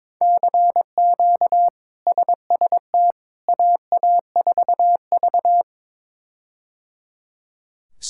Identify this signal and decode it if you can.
Morse code, heard as a keyed tone, about 22 wpm, 700 Hz. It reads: CQ SST AA4V
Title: K1USN SST QSO Practice - 22wpm